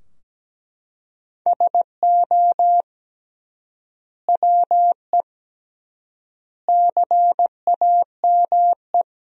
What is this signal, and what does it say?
Morse code, heard as a keyed tone, about 17 wpm, 700 Hz. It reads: SO WE CAME
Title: 3x Word-Spacing - Sentences from Top 400 Words - 17wpm